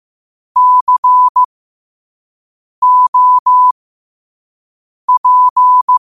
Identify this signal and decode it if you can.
Morse code, heard as a keyed tone, about 15 wpm, spaced as for 5 wpm, 1000 Hz. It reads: COP
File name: qotdfiles / 5wpm